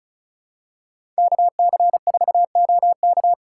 Morse code, heard as a keyed tone, about 36 wpm, 700 Hz. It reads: KC4OK
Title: USA Call Signs - 5 Characters - 3x Character Spacing - 35wpm